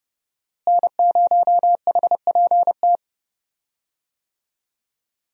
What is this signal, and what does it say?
Morse code, heard as a keyed tone, about 30 wpm, 700 Hz. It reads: N0HPT
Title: USA Call Signs - 5 Characters - Repeated 3x - 30wpm